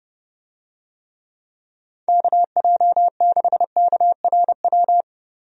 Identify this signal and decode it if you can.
Morse code, heard as a keyed tone, about 29 wpm, 700 Hz. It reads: KJ6KRW
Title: USA Call Signs - 6 Characters - Repeated 3x - 30wpm